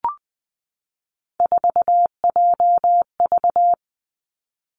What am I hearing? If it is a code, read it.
Morse code, heard as a keyed tone, about 20 wpm, 700 Hz. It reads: 4JV
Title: Three Challenging Letter-Number Combinations 20wpm